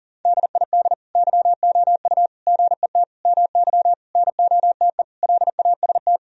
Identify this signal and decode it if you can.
Morse code, heard as a keyed tone, about 40 wpm, 700 Hz. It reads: DID YOU GET MY NOTE LAST
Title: Sentences from Top 1000 Words - Review Part 2 - 40wpm